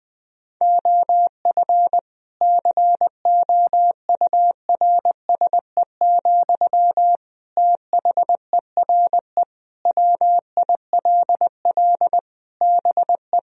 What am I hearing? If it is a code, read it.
Morse code, heard as a keyed tone, about 20 wpm, 700 Hz. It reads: OF COURSE, THERE WILL BE
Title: Sentences from Top 1000 Words - Review Part 4 - 20wpm